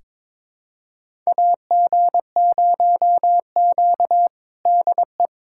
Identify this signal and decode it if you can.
Morse code, heard as a keyed tone, about 22 wpm, 700 Hz. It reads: AG0Q DE
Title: USA Callsign DE Callsign - 4 Characters - 22wpm